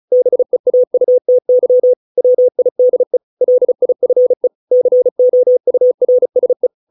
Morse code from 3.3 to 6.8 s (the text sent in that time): LIFE COURSE